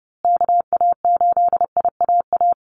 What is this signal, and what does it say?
Morse code, heard as a keyed tone, about 30 wpm, 700 Hz. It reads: KA8IAA
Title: USA Call Signs Repeated 3x 30wpm